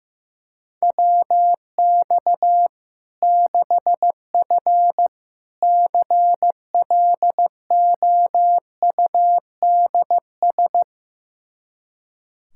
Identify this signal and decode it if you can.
Morse code, heard as a keyed tone, about 15 wpm, 700 Hz. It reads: WX 6F CLOUDS
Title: Practice QSO - Exchange 2 - Info - 15wpm